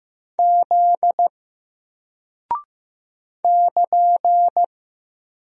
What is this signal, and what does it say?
Morse code, heard as a keyed tone, about 15 wpm, 700 Hz. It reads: Z <KN>
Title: Single Character (Letter:Number:Punctuation:Prosign) 15wpm